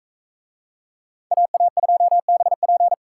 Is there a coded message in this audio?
Yes: Morse code from 1.3 to 2.9 s, heard as a keyed tone, about 40 wpm, 700 Hz. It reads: AA2BP